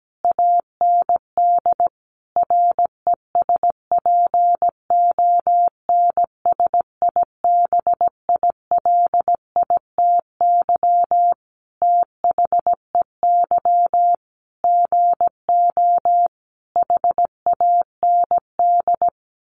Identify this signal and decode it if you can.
Morse code, heard as a keyed tone, about 17 wpm, 700 Hz. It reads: AND RESPONSIBILITY THEY GO HAND
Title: Sentences from Top 1200 Words 17wpm